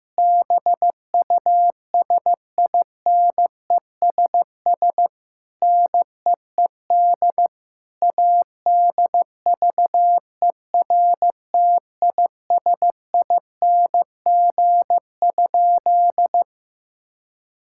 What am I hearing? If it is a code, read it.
Morse code, heard as a keyed tone, about 15 wpm, 700 Hz. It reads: BUSINESS NEED ADVERTISING?
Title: Sentences from Top 1000 Words - Review Part 6 - 15wpm